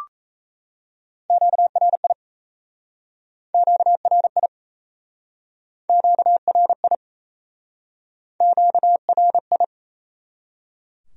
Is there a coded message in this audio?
Yes: Morse code from 1.3 to 9.6 s, heard as a keyed tone, about 37 wpm, 700 Hz. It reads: QRI QRI QRI QRI